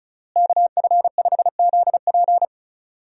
Morse code, heard as a keyed tone, about 34 wpm, 700 Hz. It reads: KF5ZP
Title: USA Call Signs - 5 Characters - Repeated 3x - 35wpm